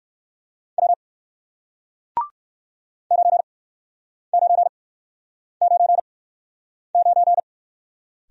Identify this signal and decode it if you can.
Morse code, heard as a keyed tone, about 57 wpm, spaced as for 9 wpm, 700 Hz. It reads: W 9999